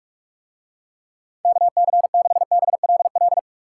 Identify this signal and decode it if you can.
Morse code, heard as a keyed tone, about 46 wpm, 690 Hz. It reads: KC6BLL